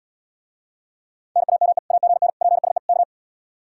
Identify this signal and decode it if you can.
Morse code, heard as a keyed tone, about 79 wpm, 700 Hz. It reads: NAME AGN PSE K